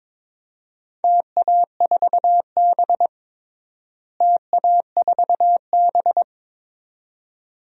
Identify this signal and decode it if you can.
Morse code, heard as a keyed tone, about 22 wpm, 700 Hz. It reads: TA4B TA4B
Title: DXCC-100 Call Signs - 4 Characters - Repeated 2x - 22wpm